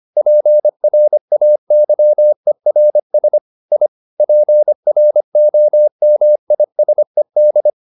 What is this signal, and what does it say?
Morse code, heard as a keyed tone, about 25 wpm, 600 Hz. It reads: PRAYERS I PROMISED